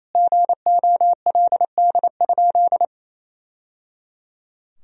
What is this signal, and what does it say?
Morse code, heard as a keyed tone, about 28 wpm, 700 Hz. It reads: GOLD?
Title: Sentences from Top 1000 Words - Review Part 3 - 28wpm